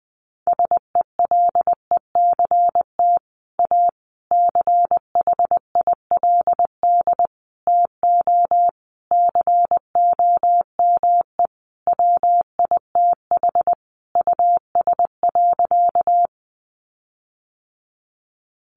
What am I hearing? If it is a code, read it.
Morse code, heard as a keyed tone, about 20 wpm, 700 Hz. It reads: SELECT A CHILD TO COME WITH US.
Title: Sentences from Top 1300 Words 20wpm